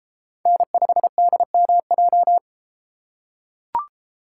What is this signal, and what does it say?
Morse code, heard as a keyed tone, about 32 wpm, 700 Hz. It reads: N5DMJ